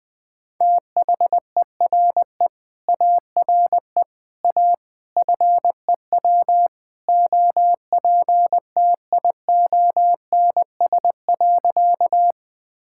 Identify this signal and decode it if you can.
Morse code, heard as a keyed tone, about 20 wpm, 700 Hz. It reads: THERE ARE A FEW OPTIONS.
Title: Sentences with 3-Word Phrases - Spoken Beginning - 20wpm